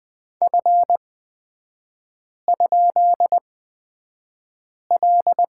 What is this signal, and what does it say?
Morse code, heard as a keyed tone, about 20 wpm, spaced as for 5 wpm, 700 Hz. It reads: F?L
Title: 255 Lesson 41e - Period - Three Characters 20wpm